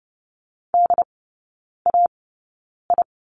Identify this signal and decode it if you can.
Morse code, heard as a keyed tone, about 30 wpm, spaced as for 9 wpm, 700 Hz. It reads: DAI
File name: 081 Lesson 14e - Number 2 - Three Characters 30wpm